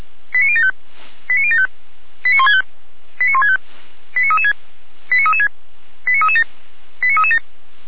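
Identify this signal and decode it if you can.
Five-tone selective call (ZVEI), heard as quick runs of five tones, tones of about 70 ms. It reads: 89075 89075 89165 89165 89336 89336 89337 89337
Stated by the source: ZVEI five-tone selective call received on an SDR.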